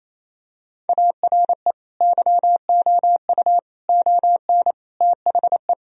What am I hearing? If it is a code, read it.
Morse code, heard as a keyed tone, about 28 wpm, 690 Hz. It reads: ARE YOU ON THE
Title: Sentences from Top 1000 Words - Review Part 2 - 28wpm